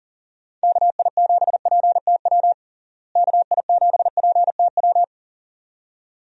Morse code, heard as a keyed tone, about 37 wpm, 700 Hz. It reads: KI7PTW KI7PTW
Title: USA Call Signs - 6 Characters - Repeated 2x - 40wpm